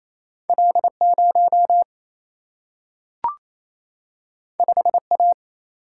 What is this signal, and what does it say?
Morse code, heard as a keyed tone, about 28 wpm, 700 Hz. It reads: L 0 5 A
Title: Single-Letter Single-Number 28wpm